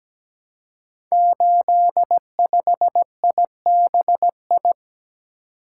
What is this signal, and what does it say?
Morse code, heard as a keyed tone, about 17 wpm, 700 Hz. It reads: 85IBI
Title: Code Groups - 5 Letter:Numbers - 17wpm